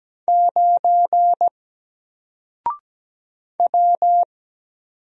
Morse code, heard as a keyed tone, about 17 wpm, 700 Hz. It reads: 9 W
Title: Single Letter-Number 17wpm